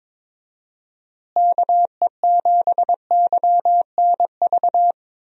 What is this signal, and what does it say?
Morse code, heard as a keyed tone, about 22 wpm, 700 Hz. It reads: KE7YNV
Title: USA Call Signs - 6 Characters - Repeated 3x - 22wpm